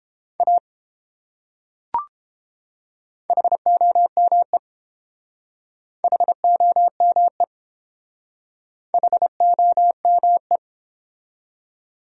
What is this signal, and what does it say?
Morse code, heard as a keyed tone, about 28 wpm, 700 Hz. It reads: A HOME HOME HOME
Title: Speed-Racing Top 200 Words 22wpm